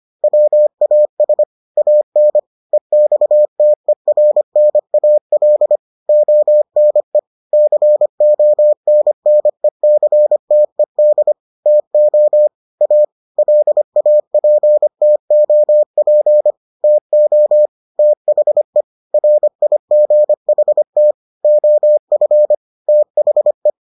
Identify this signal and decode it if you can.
Morse code, heard as a keyed tone, about 25 wpm, 600 Hz. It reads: WAS AN EXTERNAL ONE CONNECTED TO A LAPTOP TO THE RIGHT OF THE